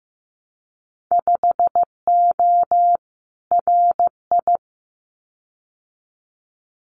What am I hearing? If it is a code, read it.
Morse code, heard as a keyed tone, about 15 wpm, 700 Hz. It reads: 5O RI
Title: Winter Field Day Exchanges Encore 15wpm